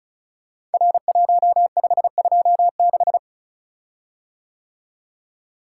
Morse code, heard as a keyed tone, about 36 wpm, 700 Hz. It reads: R1526
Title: Code Groups - 5 Letter:Numbers - 35wpm